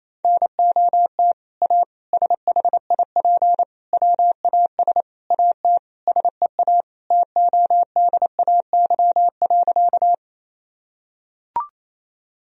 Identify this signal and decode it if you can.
Morse code, heard as a keyed tone, about 28 wpm, 700 Hz. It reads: NOT A SHIP WAS AT SEA TODAY.
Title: Sentences from Top 1000 Words - Review Part 2 - 28wpm